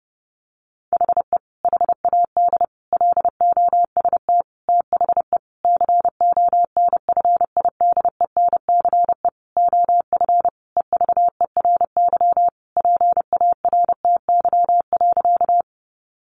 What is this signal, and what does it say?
Morse code, heard as a keyed tone, about 30 wpm, 700 Hz. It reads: HE HAD LOST THE CONFIDENCE OF EVERY PARTY.
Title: Sentences from Top 1500 Words 30wpm